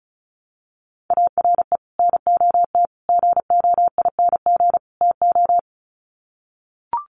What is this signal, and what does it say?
Morse code, heard as a keyed tone, about 34 wpm, 700 Hz. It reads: ARE NOT GOING TO
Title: Sets of 4 Words 35wpm